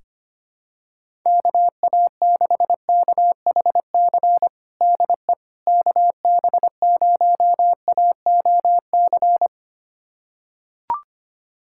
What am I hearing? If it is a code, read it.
Morse code, heard as a keyed tone, about 25 wpm, 700 Hz. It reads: KA6KHC DE KB0AOC
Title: USA Callsign DE Callsign - 6 Characters - 25wpm